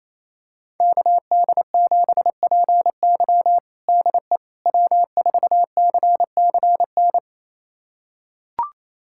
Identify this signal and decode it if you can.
Morse code, heard as a keyed tone, about 27 wpm, 700 Hz. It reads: KD7PY DE W4CCN
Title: USA Callsign DE Callsign - 5 Characters - 28wpm